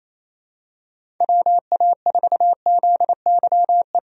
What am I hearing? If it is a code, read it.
Morse code, heard as a keyed tone, about 28 wpm, 700 Hz. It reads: WA4ZYE
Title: USA Call Signs - 6 Characters - 28wpm